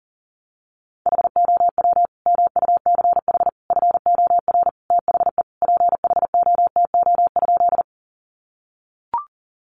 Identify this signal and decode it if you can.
Morse code, heard as a keyed tone, about 40 wpm, 700 Hz. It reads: HOW MUCH FOR THE PHOTO?
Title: Sentences from Top 1200 Words 40wpm